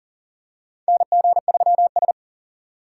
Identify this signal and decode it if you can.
Morse code, heard as a keyed tone, about 37 wpm, 700 Hz. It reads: NG3S
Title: USA Call Signs - 4 Characters - 3x Character Spacing - 40wpm